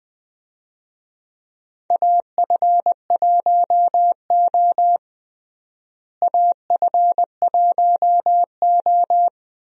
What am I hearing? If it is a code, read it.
Morse code, heard as a keyed tone, about 21 wpm, 700 Hz. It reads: AF1O AF1O
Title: USA Call Signs - 4 Characters - Repeated 2x - 20wpm